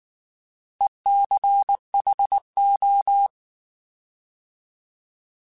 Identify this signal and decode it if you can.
Morse code, heard as a keyed tone, about 19 wpm, 800 Hz. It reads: ECHO
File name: Alphabetical / Alphabetical mp3s / Es @18